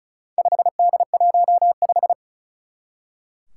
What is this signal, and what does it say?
Morse code, heard as a keyed tone, about 36 wpm, 700 Hz. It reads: 5D15
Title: Four Challenging Letter-Number Combinations 35wpm